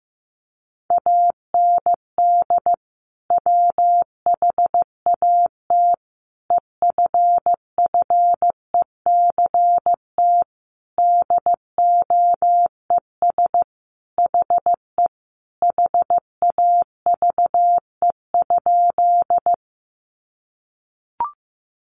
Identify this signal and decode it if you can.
Morse code, heard as a keyed tone, about 15 wpm, 700 Hz. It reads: AND WHAT EFFECT DOES HE HAVE?
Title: Sentences from Top 1000 Words 15wpm